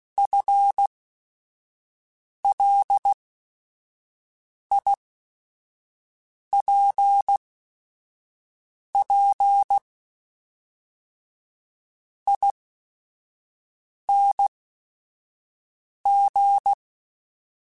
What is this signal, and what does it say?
Morse code, heard as a keyed tone, about 16 wpm, spaced as for 5 wpm, 800 Hz. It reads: FLIPP ING
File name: FL / FL mp3s / Fast FL @5